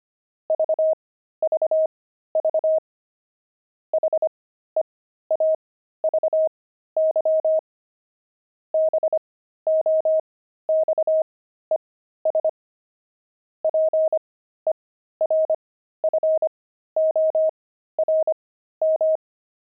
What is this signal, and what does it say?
Morse code, heard as a keyed tone, about 25 wpm, spaced as for 13 wpm, 630 Hz. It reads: VVV HEAVY BOXES PERFORM
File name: pangram-fs-13-cs-25-heavy